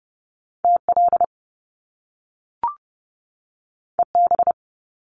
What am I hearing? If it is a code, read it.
Morse code, heard as a keyed tone, about 30 wpm, 700 Hz. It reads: TL EB
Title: Top Two Letter Combinations in Words 30wpm